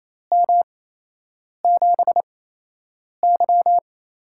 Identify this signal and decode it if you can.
Morse code, heard as a keyed tone, about 28 wpm, spaced as for 8 wpm, 700 Hz. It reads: M7Y
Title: USA Call Signs - 4 Characters - 8x Character Spacing - 28wpm